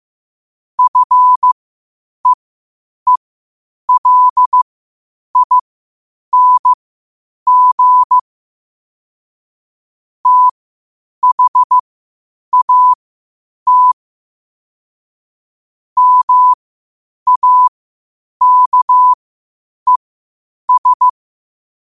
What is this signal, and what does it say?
Morse code, heard as a keyed tone, about 15 wpm, spaced as for 8 wpm, 1000 Hz. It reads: FEELING THAT MAKES
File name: qotdfiles / 8wpm